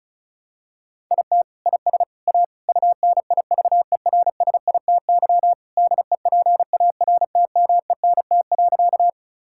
Morse code, heard as a keyed tone, about 36 wpm, 700 Hz. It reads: IT IS A UNIVERSITY DEPARTMENT.
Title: Sentences from Top 1000 Words - Review Part 5 - 35wpm